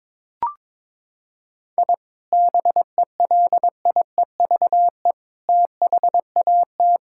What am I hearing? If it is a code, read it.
Morse code, heard as a keyed tone, about 22 wpm, 700 Hz. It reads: I BELIEVE THAT
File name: practice / 22wpm / Sentences with 4-Word Phrases - 22wpm